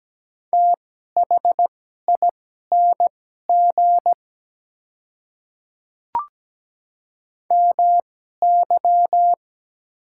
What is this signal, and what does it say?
Morse code, heard as a keyed tone, about 17 wpm, spaced as for 12 wpm, 700 Hz. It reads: THING MY
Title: 2x Character Spacing - Top 100 Words - 17wpm